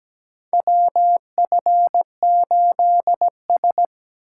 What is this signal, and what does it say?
Morse code, heard as a keyed tone, about 17 wpm, 700 Hz. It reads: WF8S
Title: USA Call Signs - 4 Characters - 6x Character Spacing - 17wpm